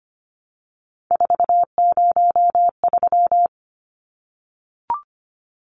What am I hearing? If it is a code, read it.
Morse code, heard as a keyed tone, about 25 wpm, 700 Hz. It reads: 403
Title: Numbers - 3-Digit 25wpm